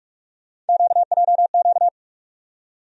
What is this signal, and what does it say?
Morse code, heard as a keyed tone, about 42 wpm, 700 Hz. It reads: QJQ
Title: USA Call Sign Postfixes - 3 Letters 45wpm